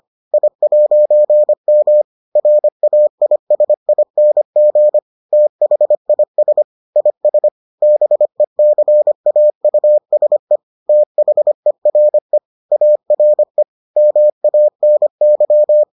Morse code, heard as a keyed tone, about 25 wpm, 600 Hz. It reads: I'M RAISING THIS IS BECAUSE THERE ARE MANY